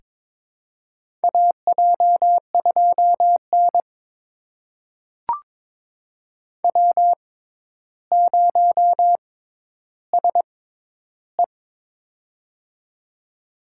Morse code, heard as a keyed tone, about 22 wpm, spaced as for 11 wpm, 700 Hz. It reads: AJ2N W0SE
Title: USA Call Signs - 4 Characters - 6x Character Spacing - 22wpm